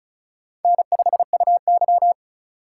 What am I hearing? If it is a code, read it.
Morse code, heard as a keyed tone, about 36 wpm, 700 Hz. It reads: N5UY
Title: USA Call Signs - 4 Characters - 3x Character Spacing - 35wpm